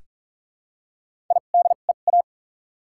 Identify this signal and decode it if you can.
Morse code, heard as a keyed tone, about 46 wpm, 700 Hz. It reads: I D E A
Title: 2x Character Spacing - Top 300 Words - 45wpm